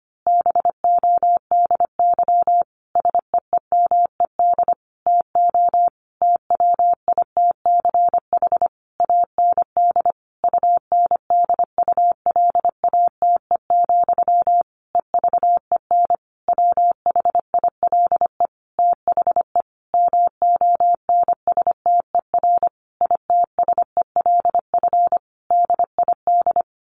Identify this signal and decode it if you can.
Morse code, heard as a keyed tone, about 25 wpm, 700 Hz. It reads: BODY SEEMED TO TWITCH AND UNDULATE, EVEN WHILE THE MONSTER ITSELF DID
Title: A Sound Of Thunder 25wpm